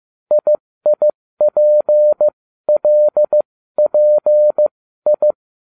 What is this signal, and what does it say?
Morse code, heard as a keyed tone, about 15 wpm, spaced as for 13 wpm, 600 Hz. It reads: IIPLPI